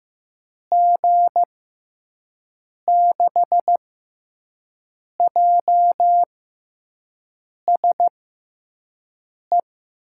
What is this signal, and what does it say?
Morse code, heard as a keyed tone, about 15 wpm, spaced as for 5 wpm, 700 Hz. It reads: G6JSE
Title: USA Call Signs - 6 Characters - 6x Character Spacing - 15wpm